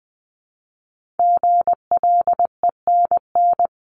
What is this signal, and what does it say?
Morse code, heard as a keyed tone, about 20 wpm, 700 Hz. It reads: GLENN
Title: CWT Exchange 20wpm